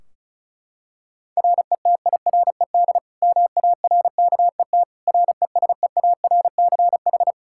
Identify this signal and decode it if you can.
Morse code, heard as a keyed tone, about 36 wpm, 700 Hz. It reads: RETIRED MARKET RESEARCH
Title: Practice QSO - Exchange 3 - More Info - 35wpm